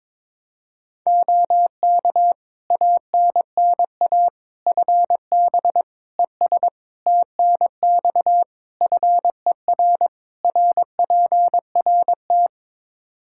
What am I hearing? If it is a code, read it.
Morse code, heard as a keyed tone, about 22 wpm, 700 Hz. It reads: OK ANNA FB ES TNX FER RPRT
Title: Practice QSO - Exchange 2 - Info - 22wpm